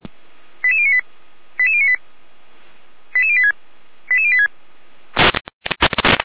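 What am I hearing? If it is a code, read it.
Five-tone selective call (ZVEI), heard as quick runs of five tones, tones of about 70 ms. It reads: 88098 88098 88086 88086
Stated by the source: ZVEI five-tone selective call received on an SDR.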